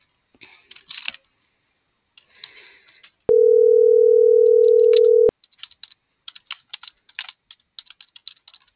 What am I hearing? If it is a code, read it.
Ringback tone (North America), 440 + 480 Hz.